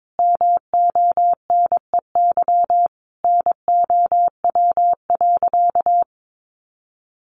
Morse code, heard as a keyed tone, about 22 wpm, 700 Hz. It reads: MONEY NOW.
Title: Sentences from Top 1300 Words 22wpm